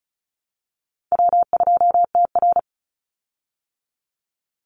Morse code, heard as a keyed tone, about 34 wpm, 700 Hz. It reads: W2TR
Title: Easy Call Signs Repeated 2x - 35wpm